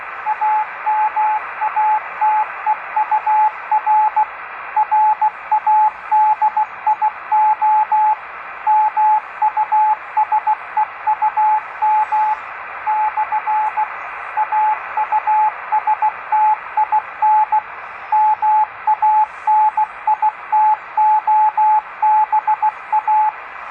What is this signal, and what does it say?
Morse code, heard as a keyed tone, about 16 wpm, 860 Hz, with radio noise. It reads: AMATEUR RADIO MUSEUM / AUSTIN MANITOBA
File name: tracks / Amateur Radio Museum